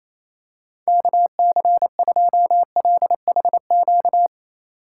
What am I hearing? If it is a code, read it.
Morse code, heard as a keyed tone, about 28 wpm, 700 Hz. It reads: KC2LHQ
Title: USA Call Signs - 6 Characters - 6x Character Spacing - 28wpm